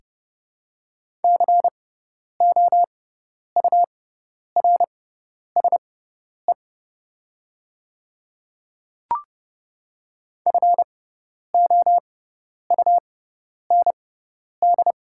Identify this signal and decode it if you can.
Morse code, heard as a keyed tone, about 30 wpm, spaced as for 10 wpm, 700 Hz. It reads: COURSE FOUND
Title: 6x Character Spacing - Top 500 Words - 30wpm